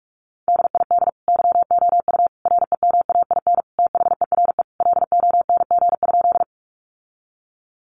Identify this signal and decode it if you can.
Morse code, heard as a keyed tone, about 44 wpm, 700 Hz. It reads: DID YOU REMAIN THERE LONG?
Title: Sentences from Top 1200 Words 45wpm